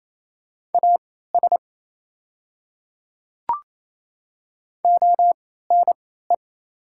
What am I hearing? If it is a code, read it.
Morse code, heard as a keyed tone, about 27 wpm, spaced as for 16 wpm, 700 Hz. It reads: AS ONE
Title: 3x Character Spacing - Top 200 Words - 28wpm